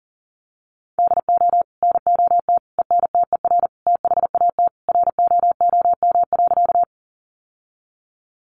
Morse code, heard as a keyed tone, about 40 wpm, 700 Hz. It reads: DO NOT ENTER THAT ROOM.
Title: Sentences from Top 1300 Words 40wpm